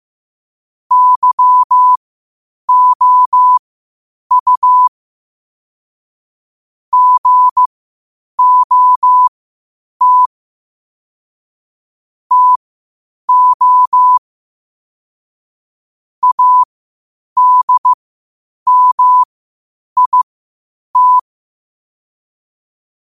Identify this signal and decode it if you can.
Morse code, heard as a keyed tone, about 15 wpm, spaced as for 8 wpm, 1000 Hz. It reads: YOU GOT TO ADMIT
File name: qotdfiles / 8wpm